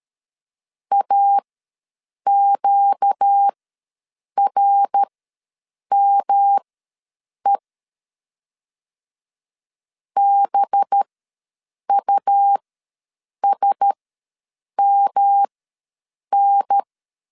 Morse code, heard as a keyed tone, about 13 wpm, spaced as for 7 wpm, 800 Hz. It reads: AQRME BUSMN
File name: 5 LB ws 12wpm